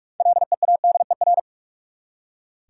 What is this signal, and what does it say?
Morse code, heard as a keyed tone, about 46 wpm, 700 Hz. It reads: LEADER
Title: Six Letter Words - 45wpm